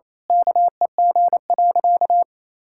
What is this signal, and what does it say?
Morse code, heard as a keyed tone, about 28 wpm, 700 Hz. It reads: KEG.
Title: Pangram Challenge v2 - 28wpm